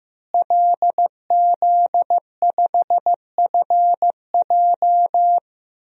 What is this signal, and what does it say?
Morse code, heard as a keyed tone, about 15 wpm, 700 Hz. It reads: LZ5FJ
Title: DXCC-100 Call Signs - 5 Characters - Repeated 2x - 15wpm